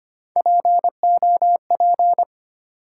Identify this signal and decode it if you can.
Morse code, heard as a keyed tone, about 25 wpm, 700 Hz. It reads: POP
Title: Three Letter Words - 25wpm